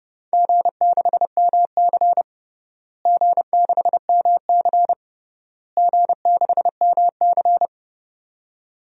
Morse code, heard as a keyed tone, about 30 wpm, 700 Hz. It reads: G6MC G6MC G6MC
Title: DXCC-100 Call Signs - 4 Characters - Repeated 3x - 30wpm